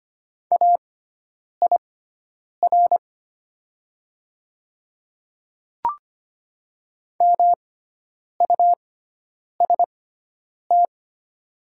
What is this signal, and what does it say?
Morse code, heard as a keyed tone, about 25 wpm, spaced as for 9 wpm, 690 Hz. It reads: AIR MUST
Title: 6x Character Spacing - Top 200 Words - 25wpm